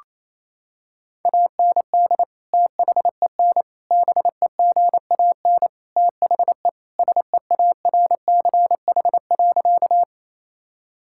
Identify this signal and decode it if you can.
Morse code, heard as a keyed tone, about 28 wpm, 700 Hz. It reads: AND THEN BEGAN THE SEARCH.
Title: Sentences from Top 1000 Words - Review Part 5 - 28wpm